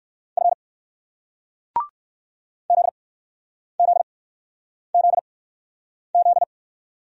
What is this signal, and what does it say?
Morse code, heard as a keyed tone, about 57 wpm, spaced as for 9 wpm, 700 Hz. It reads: F ZZZZ